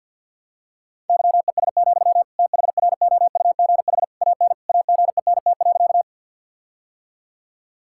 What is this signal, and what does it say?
Morse code, heard as a keyed tone, about 50 wpm, 700 Hz. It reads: YES, THROUGH AN AGENT.
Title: Sentences from Top 1600 Words 50wpm